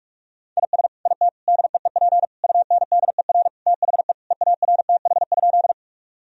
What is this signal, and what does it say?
Morse code, heard as a keyed tone, about 46 wpm, 700 Hz. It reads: IS IT DEEP UNDER THE EARTH?
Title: Sentences from Top 1000 Words - Review Part 3 - 45wpm